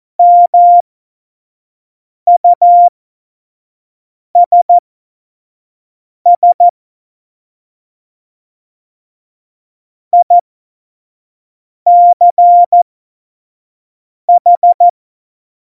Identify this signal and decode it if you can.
Morse code, heard as a keyed tone, about 14 wpm, spaced as for 5 wpm, 700 Hz. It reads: MUSS ICH
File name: Morsekurs / Witze / Witz 08. September 2020, 25 F70